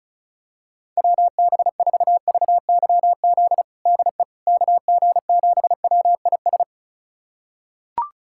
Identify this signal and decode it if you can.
Morse code, heard as a keyed tone, about 36 wpm, 690 Hz. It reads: WB4VYZ DE KG7WIS
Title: USA Callsign DE Callsign 35wpm